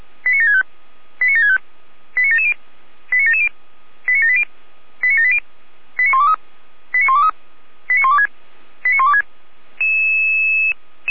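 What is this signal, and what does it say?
Five-tone selective call (ZVEI), heard as quick runs of five tones, tones of about 70 ms. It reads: 89765 89765 89770 89770 89780 89780 89123 89123 89126 89126
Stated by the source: ZVEI five-tone selective call received on an SDR.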